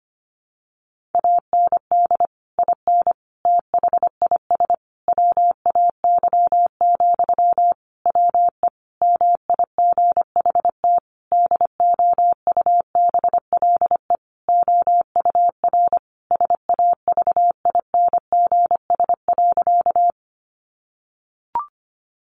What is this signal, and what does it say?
Morse code, heard as a keyed tone, about 25 wpm, 700 Hz. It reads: AND IN THIS WAY, WE MIGHT DOUBLE OUR SAVINGS.
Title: Sentences from Top 1400 Words 25wpm